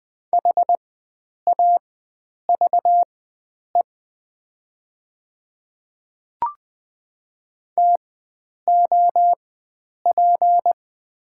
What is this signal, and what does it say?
Morse code, heard as a keyed tone, about 20 wpm, spaced as for 9 wpm, 700 Hz. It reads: HAVE TOP
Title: 4x Character Spacing - Top 500 Words - 20wpm